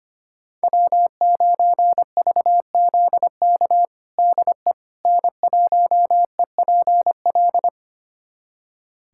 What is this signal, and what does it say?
Morse code, heard as a keyed tone, about 25 wpm, 700 Hz. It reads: W9VZK DE N1EPL
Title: USA Callsign DE Callsign - 5 Characters - 25wpm